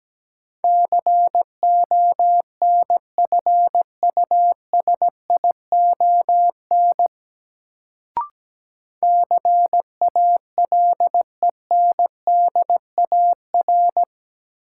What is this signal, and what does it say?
Morse code, heard as a keyed tone, about 17 wpm, 700 Hz. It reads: CONFUSION CALENDAR
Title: Sentences from Top 1600 Words - New Word Drill - 17wpm